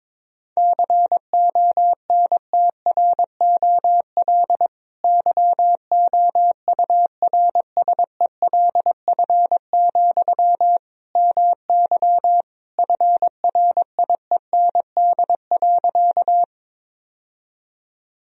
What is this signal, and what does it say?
Morse code, heard as a keyed tone, about 22 wpm, 700 Hz. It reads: CONTROL YOURSELF, MY FRIEND.
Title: Sentences from Top 1000 Words - Review Part 3 - 22wpm